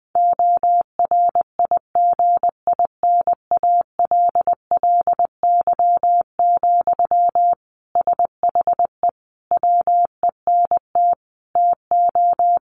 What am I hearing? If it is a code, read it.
Morse code, heard as a keyed tone, about 20 wpm, 700 Hz. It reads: ORIGINALLY, SHE WENT TO
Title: Sentences from Top 1500 Words 20wpm